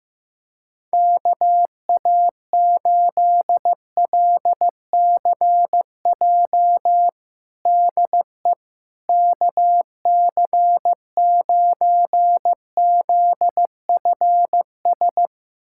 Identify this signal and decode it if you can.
Morse code, heard as a keyed tone, about 16 wpm, 700 Hz. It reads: KA8LCJ DE KC9ZFS
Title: USA Callsign DE Callsign - 6 Characters - 15wpm